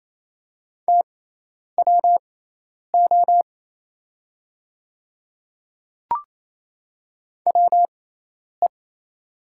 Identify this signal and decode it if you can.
Morse code, heard as a keyed tone, about 28 wpm, spaced as for 10 wpm, 700 Hz. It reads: TWO WE
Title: 6x Character Spacing - Top 200 Words - 28wpm